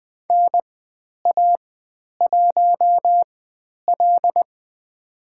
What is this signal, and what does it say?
Morse code, heard as a keyed tone, about 20 wpm, spaced as for 10 wpm, 700 Hz. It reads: NA1L
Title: 109 Lesson 18g - Letter m - Call Signs 20wpm